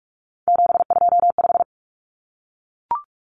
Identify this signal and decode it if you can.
Morse code, heard as a keyed tone, about 44 wpm, 700 Hz. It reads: 725
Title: Common Cut Numbers - 3-Digit 45wpm